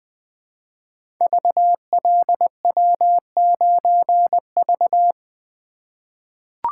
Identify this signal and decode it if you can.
Morse code, heard as a keyed tone, about 20 wpm, 700 Hz. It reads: VLW9V
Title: Code Groups - 5 Letter:Numbers - 20wpm